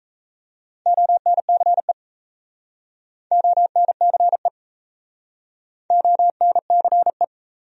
Morse code, heard as a keyed tone, about 37 wpm, 700 Hz. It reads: ONCE ONCE ONCE